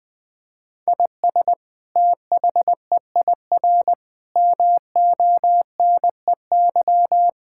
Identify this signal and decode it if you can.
Morse code, heard as a keyed tone, about 20 wpm, 700 Hz. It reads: IS THEIR MONEY
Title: Sentences from Top 1600 Words 20wpm